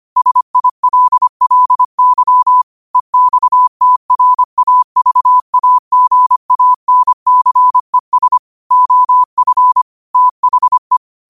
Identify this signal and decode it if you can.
Morse code, heard as a keyed tone, about 25 wpm, 1000 Hz. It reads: SILLY EXTRAVAGANCES OF THE
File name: qotdfiles / 25wpm